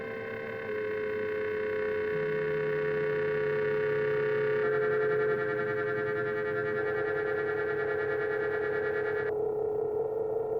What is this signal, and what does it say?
Dial tone (Europe, 425 Hz), with noise.